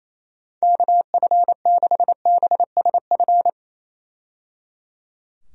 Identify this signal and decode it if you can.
Morse code, heard as a keyed tone, about 28 wpm, 700 Hz. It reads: KF6BSF
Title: USA Call Signs - 6 Characters - 28wpm